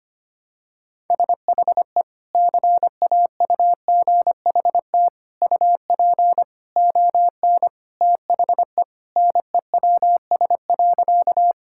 Morse code, heard as a keyed tone, about 25 wpm, 700 Hz. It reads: SHE CAUGHT UP ON THE NEWS.
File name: practice / 25wpm / Sentences with 3-Word Phrases - 25wpm